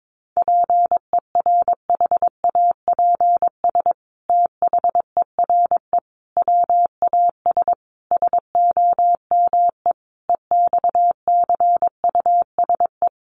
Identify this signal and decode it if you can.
Morse code, heard as a keyed tone, about 22 wpm, 700 Hz. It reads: PERHAPS THERE WAS SOME EXCUSE
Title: Sentences from Top 700 Words 22wpm